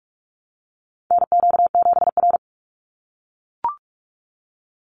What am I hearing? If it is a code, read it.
Morse code, heard as a keyed tone, about 44 wpm, 690 Hz. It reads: NQ7R
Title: NA SOTA Chaser Callsigns 45wpm